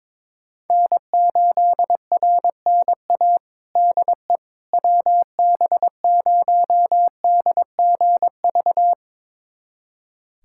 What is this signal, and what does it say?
Morse code, heard as a keyed tone, about 22 wpm, 700 Hz. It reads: N8RNA DE WB0DGV
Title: USA Callsign DE Callsign 22wpm